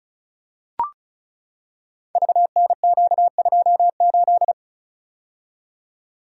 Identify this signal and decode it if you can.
Morse code, heard as a keyed tone, about 36 wpm, 690 Hz. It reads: VNQ28
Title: Code Groups - 5 Letter:Numbers - 35wpm